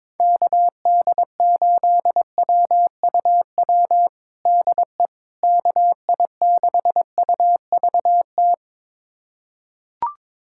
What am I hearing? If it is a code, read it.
Morse code, heard as a keyed tone, about 22 wpm, 700 Hz. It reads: KD8WUW DE KI6UVT
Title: USA Callsign DE Callsign - 6 Characters - 22wpm